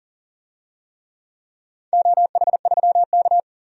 Morse code, heard as a keyed tone, about 40 wpm, 700 Hz. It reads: OH3K
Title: DXCC-100 Call Signs - 4 Characters - 40wpm